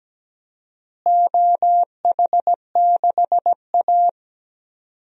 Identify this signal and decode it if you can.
Morse code, heard as a keyed tone, about 17 wpm, 700 Hz. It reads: OH6A
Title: DXCC-100 Call Signs - 4 Characters - Repeated 2x - 17wpm